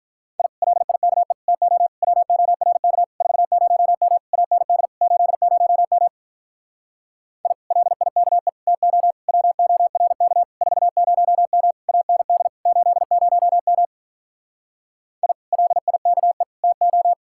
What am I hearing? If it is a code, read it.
Morse code, heard as a keyed tone, about 48 wpm, 700 Hz. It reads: I LIKE TO WORK 40M AND 80M I LIKE TO WORK 40M AND 80M I LIKE TO